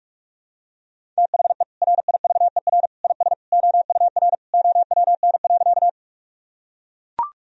Morse code, heard as a keyed tone, about 41 wpm, 700 Hz. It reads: THE RIVER IS OUR OWN.
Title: Sentences from Top 1000 Words - Review Part 1 - 45wpm